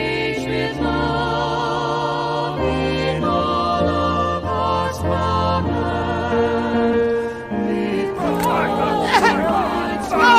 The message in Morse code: none